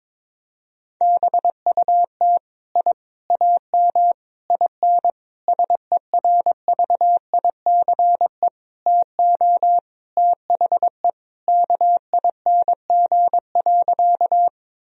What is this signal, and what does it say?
Morse code, heard as a keyed tone, about 22 wpm, 700 Hz. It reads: BUT I AM IN SERVICE TO THE KING.
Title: Sentences from Top 1000 Words - Review Part 4 - 22wpm